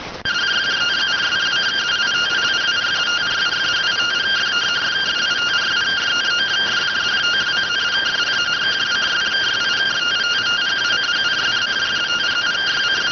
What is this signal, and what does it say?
Two-tone teleprinter signal, about 170 Hz shift.